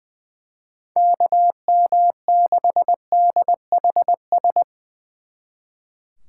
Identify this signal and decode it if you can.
Morse code, heard as a keyed tone, about 23 wpm, 700 Hz. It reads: KM6DHS
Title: Speed Racing - USA 2x3 Call Signs 20wpm